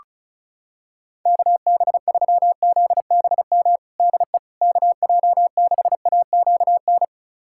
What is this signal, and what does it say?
Morse code, heard as a keyed tone, about 36 wpm, 700 Hz. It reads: KB3ZBM DE KJ6AQN
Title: USA Callsign DE Callsign - 6 Characters - 35wpm